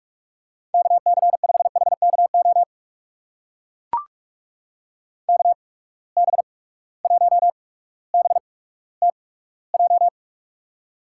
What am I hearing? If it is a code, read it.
Morse code, heard as a keyed tone, about 45 wpm, spaced as for 19 wpm, 700 Hz. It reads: KC5HKO KB1BTJ
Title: USA Call Signs - 6 Characters - 8x Character Spacing - 45wpm